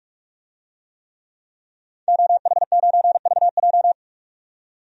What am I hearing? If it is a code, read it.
Morse code, heard as a keyed tone, about 46 wpm, 700 Hz. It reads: OH9VJ
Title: DXCC-100 Call Signs - 5 Characters - 45wpm